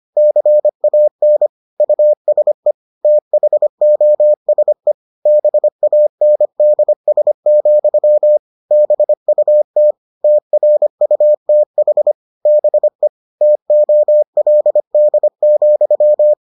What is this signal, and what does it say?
Morse code, heard as a keyed tone, about 25 wpm, 600 Hz. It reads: CAN USE THOSE BANDS, BUT TRUTH BE TOLD,